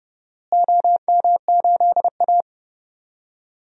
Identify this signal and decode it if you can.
Morse code, heard as a keyed tone, about 30 wpm, 700 Hz. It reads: OM8A
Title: DXCC-100 Call Signs - 4 Characters - Repeated 2x - 30wpm